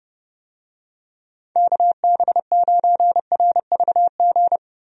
Morse code, heard as a keyed tone, about 30 wpm, 700 Hz. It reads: KB9RVG
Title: USA Call Signs - 6 Characters - Repeated 2x - 30wpm